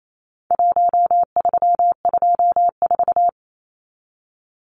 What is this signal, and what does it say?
Morse code, heard as a keyed tone, about 28 wpm, 700 Hz. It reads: 1324
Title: Numbers - Contest Style - 4-Digit 28wpm